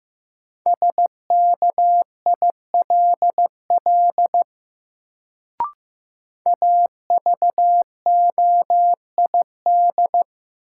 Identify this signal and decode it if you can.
Morse code, heard as a keyed tone, about 15 wpm, 700 Hz. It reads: SKILL AVOID
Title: Sentences from Top 1200 Words - New Word Drill Alt - 15wpm